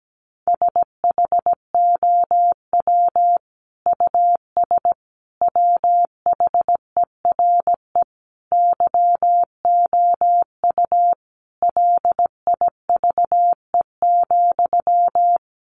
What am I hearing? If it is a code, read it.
Morse code, heard as a keyed tone, about 17 wpm, 700 Hz. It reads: SHOW US WHERE YOU LIVE,
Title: Sentences from Top 200 Words 17wpm